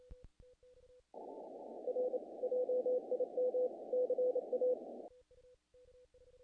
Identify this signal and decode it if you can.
Morse code, heard as a keyed tone, about 27 wpm, 500 Hz. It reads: R JIM CA
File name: contests / naqp-cw-k5tr-op-k5pi- -1800